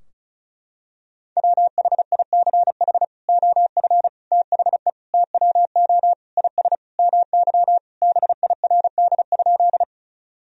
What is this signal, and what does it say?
Morse code, heard as a keyed tone, about 36 wpm, 700 Hz. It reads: WHICH OF THE TWO IS MY BIRD?
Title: Sentences from Top 1000 Words - Review Part 2 - 35wpm